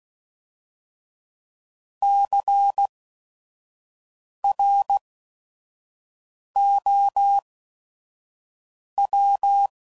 Morse code, heard as a keyed tone, about 16 wpm, spaced as for 5 wpm, 800 Hz. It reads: CROW
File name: Prefixes / By First Letter / Cs